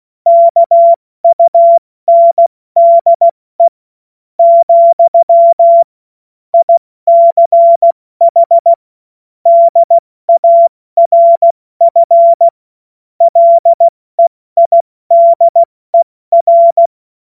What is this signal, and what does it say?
Morse code, heard as a keyed tone, about 16 wpm, spaced as for 13 wpm, 680 Hz. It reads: KUNDE, ICH DARF LEIDER
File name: Morsekurs / Witze / Witz 13. Juli 2021, 70 F80